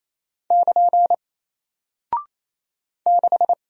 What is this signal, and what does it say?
Morse code, heard as a keyed tone, about 27 wpm, 700 Hz. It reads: <KN> 6
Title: Single Character (Letter:Number:Punctuation:Prosign) Alt 28wpm